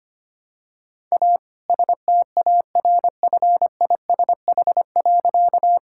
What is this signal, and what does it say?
Morse code, heard as a keyed tone, about 25 wpm, 700 Hz. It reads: A STARFISH.
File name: practice / 25wpm / All-Age Appropriate Jokes - 25wpm